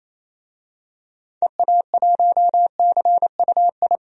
Morse code, heard as a keyed tone, about 28 wpm, 700 Hz. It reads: EA1CUI
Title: DXCC-100 Call Signs - 6 Characters - 28wpm